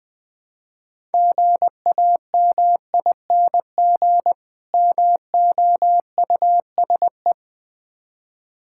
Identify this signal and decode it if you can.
Morse code, heard as a keyed tone, about 20 wpm, 700 Hz. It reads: GAMING MOUSE
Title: Difficult Words with Hints - 20wpm